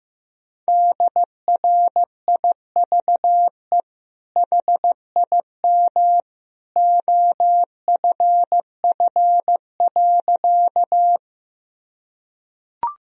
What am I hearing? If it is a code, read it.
Morse code, heard as a keyed tone, about 15 wpm, 700 Hz. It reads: DRIVE HIM OFF.
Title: Sentences from Top 1000 Words - Review Part 2 - 15wpm